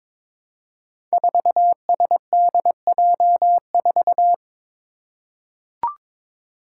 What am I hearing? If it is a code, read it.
Morse code, heard as a keyed tone, about 22 wpm, 700 Hz. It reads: 4SDJ4
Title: Five Challenging Letter-Number Combinations 22wpm